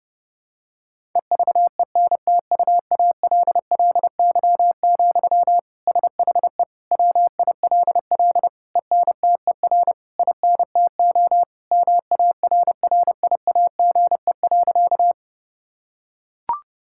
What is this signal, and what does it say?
Morse code, heard as a keyed tone, about 30 wpm, 700 Hz. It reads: EVENTUALLY, SHE WILL ENTER INTO MARRIAGE.
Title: Sentences from Top 1000 Words - Review Part 6 - 30wpm